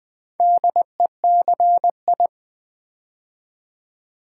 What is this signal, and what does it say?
Morse code, heard as a keyed tone, about 20 wpm, 700 Hz. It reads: DECI
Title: Build A Word - 2 Letters At A Time - 6-Letter Words - 20wpm